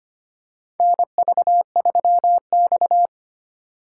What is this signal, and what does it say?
Morse code, heard as a keyed tone, about 25 wpm, 700 Hz. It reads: NV3X
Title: USA Call Signs - 4 Characters - 2x Character Spacing - 25wpm